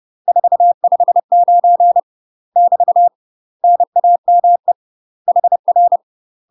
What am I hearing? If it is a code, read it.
Morse code, heard as a keyed tone, about 30 wpm, 700 Hz. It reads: 459 <BT> NAME HR
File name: UK MASTER QSOs TRAINING / 30WPM / UK MASTER QSOs 0028